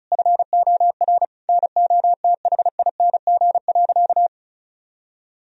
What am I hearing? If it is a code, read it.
Morse code, heard as a keyed tone, about 36 wpm, 700 Hz. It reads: FOR NOTHING.
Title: Sentences from Top 1000 Words - Review Part 2 - 35wpm